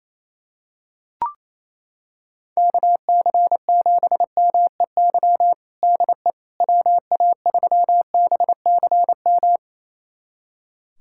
Morse code, heard as a keyed tone, about 28 wpm, 700 Hz. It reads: KC7MEY DE WA3BCM
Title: USA Callsign DE Callsign 28wpm